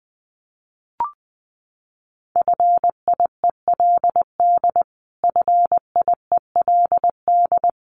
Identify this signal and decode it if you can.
Morse code, heard as a keyed tone, about 19 wpm, 700 Hz. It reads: FIELD FIELD
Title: Top 500 Words Repeated 20wpm